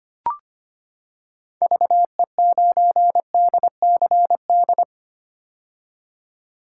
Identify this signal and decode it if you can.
Morse code, heard as a keyed tone, about 25 wpm, 700 Hz. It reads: VE9DCD
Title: K1USN SST QSO Practice - 25wpm